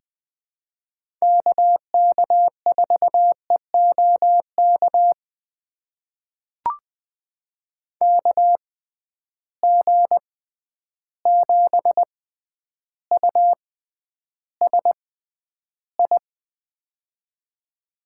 Morse code, heard as a keyed tone, about 20 wpm, spaced as for 10 wpm, 700 Hz. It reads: KK4EOK KG7USI
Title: USA Call Signs - 6x Character Spacing - 20wpm